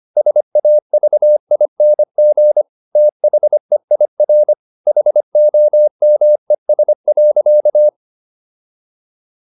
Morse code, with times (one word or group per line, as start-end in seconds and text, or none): 0.2-2.6 SAVING
3.0-4.5 THEIR
4.9-7.9 HOMES.